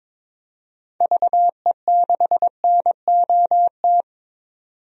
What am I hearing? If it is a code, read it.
Morse code, heard as a keyed tone, about 22 wpm, 700 Hz. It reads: VE6NOT
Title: DXCC-100 Call Signs - 6 Characters - 22wpm